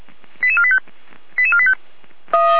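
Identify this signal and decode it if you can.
Five-tone selective call (ZVEI), heard as quick runs of five tones, tones of about 70 ms. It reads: 88485 88485
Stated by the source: Five-tone ZVEI selective call received on an SDR.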